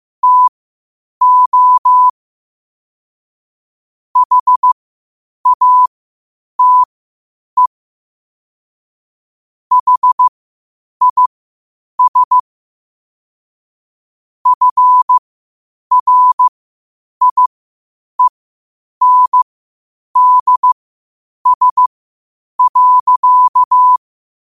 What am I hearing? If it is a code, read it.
Morse code, heard as a keyed tone, about 15 wpm, spaced as for 8 wpm, 1000 Hz. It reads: TO HATE HIS FRIENDS.
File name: qotdfiles / 8wpm